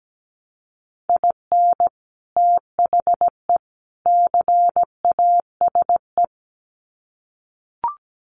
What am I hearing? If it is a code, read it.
Morse code, heard as a keyed tone, about 17 wpm, 700 Hz. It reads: IN THE CASE
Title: Sets of 3 Words 17wpm